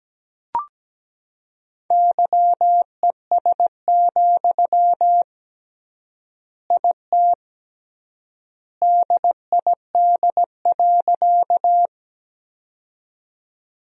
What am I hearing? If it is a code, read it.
Morse code, heard as a keyed tone, about 17 wpm, 700 Hz. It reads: YES, IT DID.
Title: 3x Word-Spacing - Sentences from Top 500 Words - 17wpm